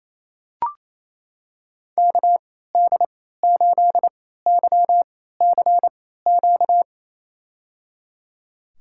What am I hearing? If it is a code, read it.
Morse code, heard as a keyed tone, about 28 wpm, spaced as for 16 wpm, 700 Hz. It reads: KD8YCQ
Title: USA Call Signs - 6 Characters - 3x Character Spacing - 28wpm